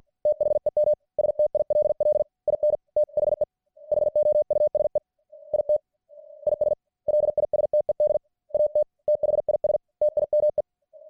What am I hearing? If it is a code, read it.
Morse code, heard as a keyed tone, about 50 wpm, 600 Hz. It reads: THEM STILL IN THE HOUSE IT IS LISTED AT THIS TIME